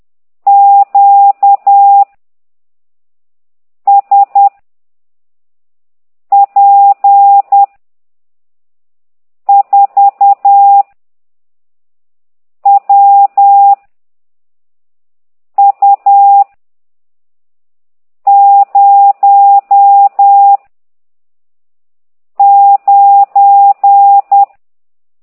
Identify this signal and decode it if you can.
Morse code, heard as a keyed tone, about 10 wpm, 800 Hz. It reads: Q S P 4 W U 0 9
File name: ct2idw / exerc15